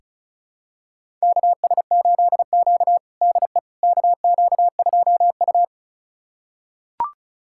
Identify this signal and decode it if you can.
Morse code, heard as a keyed tone, about 36 wpm, 700 Hz. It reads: KS8Q DE KQ2U
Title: USA Callsign DE Callsign - 4 Characters - 35wpm